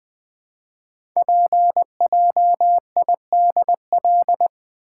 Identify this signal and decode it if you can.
Morse code, heard as a keyed tone, about 20 wpm, 700 Hz. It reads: PJIDL
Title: Code Groups - 5 Letters - 20wpm